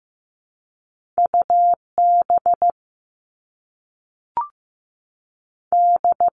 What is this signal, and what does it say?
Morse code, heard as a keyed tone, about 15 wpm, 700 Hz. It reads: U B D
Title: All Cut Numbers - 2-Digit 15wpm